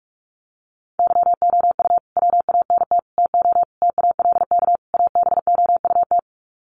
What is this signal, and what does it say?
Morse code, heard as a keyed tone, about 44 wpm, 700 Hz. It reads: YOU WANT TO TALK ABOUT
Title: Sets of 5 Words Encore 45wpm